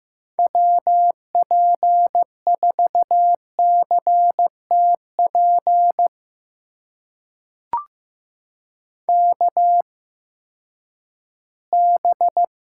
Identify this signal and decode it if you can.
Morse code, heard as a keyed tone, about 15 wpm, spaced as for 10 wpm, 700 Hz. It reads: WP4CTP KB
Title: USA Call Signs - 6 Characters - 8x Character Spacing - 15wpm